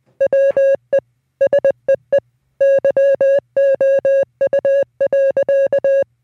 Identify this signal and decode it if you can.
Morse code, heard as a keyed tone, about 20 wpm, 550 Hz. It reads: WE SEE YOU.